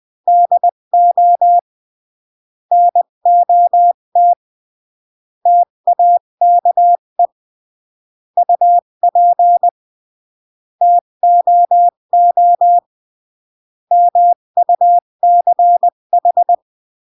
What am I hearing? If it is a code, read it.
Morse code, heard as a keyed tone, about 20 wpm, 700 Hz. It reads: DO NOT TAKE UP TOO MUCH
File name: CW / jodorowsky / 82 commandments 20-15